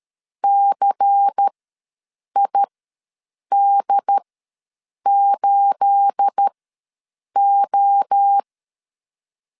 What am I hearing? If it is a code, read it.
Morse code, heard as a keyed tone, about 13 wpm, spaced as for 7 wpm, 800 Hz. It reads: CID8O
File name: LEZIONI / 9 28 ws 12wpm 3354